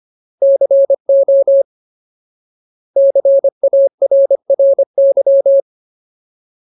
Morse code, heard as a keyed tone, about 25 wpm, 550 Hz. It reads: CO CARRY
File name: Ultimate Word List / 25WPM Ultimate